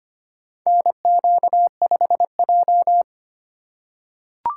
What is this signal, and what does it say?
Morse code, heard as a keyed tone, about 25 wpm, 700 Hz. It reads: NQ5J
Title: USA Call Signs - 4 Characters - 3x Character Spacing - 25wpm